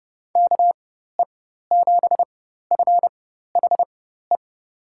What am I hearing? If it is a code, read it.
Morse code, heard as a keyed tone, about 32 wpm, spaced as for 14 wpm, 700 Hz. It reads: KE7FHE
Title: USA Call Signs - 6 Characters - 4x Character Spacing - 30wpm